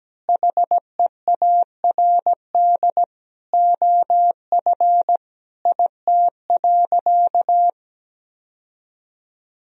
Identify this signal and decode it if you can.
Morse code, heard as a keyed tone, about 17 wpm, 700 Hz. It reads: HEARD OF IT.
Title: Sentences from Top 1000 Words - Review Part 3 - 17wpm